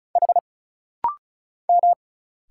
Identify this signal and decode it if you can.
Morse code, heard as a keyed tone, about 34 wpm, 700 Hz. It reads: H M
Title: Single Letter-Number Alt - Rapid-Fire 35wpm